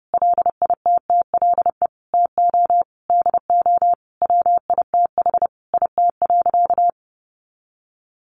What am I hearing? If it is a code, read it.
Morse code, heard as a keyed tone, about 30 wpm, 700 Hz. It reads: LITTLE TO DO WITH IT.
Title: Sentences from Top 1500 Words 30wpm